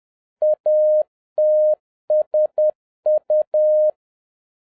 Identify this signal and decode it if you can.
Morse code, heard as a keyed tone, about 10 wpm, 610 Hz. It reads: ATSU
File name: 2011-07-mp3 / n21 GREETINGS FROM SINGAPORE 10wpm